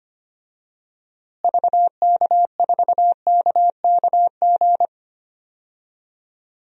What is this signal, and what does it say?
Morse code, heard as a keyed tone, about 25 wpm, 700 Hz. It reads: VK4KKG
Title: DXCC-100 Call Signs - 6 Characters - Repeated 2x - 25wpm